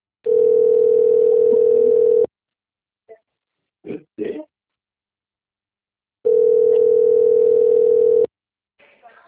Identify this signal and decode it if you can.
Ringback tone (North America), 440 + 480 Hz.
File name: rg-1600-unknown- - - .9912